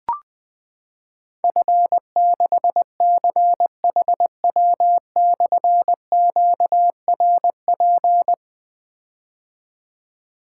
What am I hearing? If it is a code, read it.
Morse code, heard as a keyed tone, about 20 wpm, 700 Hz. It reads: F6CHW/QRP
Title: Difficult Call Signs 20wpm